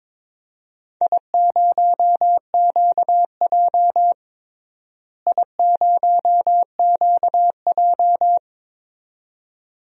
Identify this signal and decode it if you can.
Morse code, heard as a keyed tone, about 22 wpm, 700 Hz. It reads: I0QJ I0QJ
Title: DXCC-100 Call Signs - 4 Characters - Repeated 2x - 22wpm